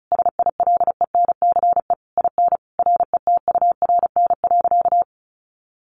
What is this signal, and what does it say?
Morse code, heard as a keyed tone, about 34 wpm, 700 Hz. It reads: SILENCE IN RETURN.
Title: A Sound Of Thunder 35wpm